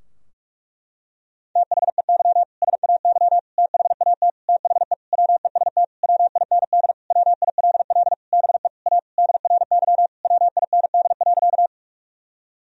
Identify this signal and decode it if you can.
Morse code, heard as a keyed tone, about 41 wpm, 700 Hz. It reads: THEY SAY THAT THE WEST WIND WILL BE A DRY WIND.
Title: Sentences from Top 1000 Words - Review Part 3 - 45wpm